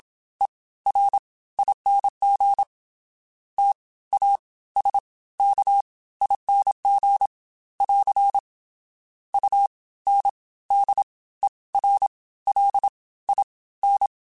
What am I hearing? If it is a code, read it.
Morse code, heard as a keyed tone, about 27 wpm, spaced as for 17 wpm, 800 Hz. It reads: ERING TASKING <AR> UNDERLIN